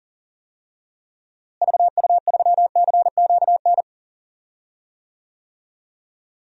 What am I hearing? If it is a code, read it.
Morse code, heard as a keyed tone, about 38 wpm, 700 Hz. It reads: VU3CQN
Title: DXCC-100 Call Signs - 6 Characters - Repeated 3x - 40wpm